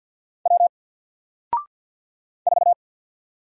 Morse code, heard as a keyed tone, about 48 wpm, 700 Hz. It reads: W 4
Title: Single Character (Letter:Number:Punctuation:Prosign) 50wpm